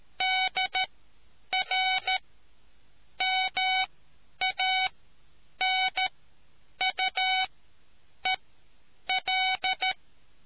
Morse code, heard as a keyed tone, about 13 wpm, spaced as for 8 wpm, 750 Hz. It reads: DR MANUEL